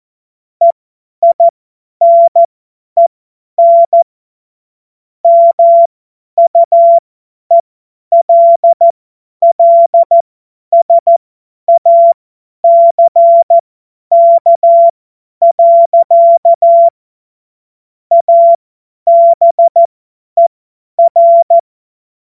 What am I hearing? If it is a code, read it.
Morse code, heard as a keyed tone, about 14 wpm, spaced as for 10 wpm, 680 Hz. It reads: EINEN MUELLSACK. ABER